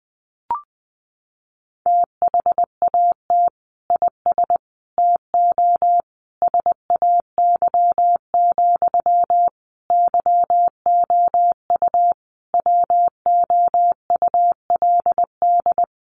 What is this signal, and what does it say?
Morse code, heard as a keyed tone, about 20 wpm, 700 Hz. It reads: THAT IS TO SAY, YOU WOULD
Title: Sentences from Top 700 Words 20wpm